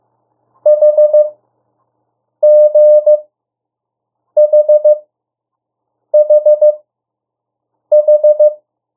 Morse code, heard as a keyed tone, about 15 wpm, spaced as for 6 wpm, 600 Hz. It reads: HGHHH
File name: morse mp3 / 02 urok 1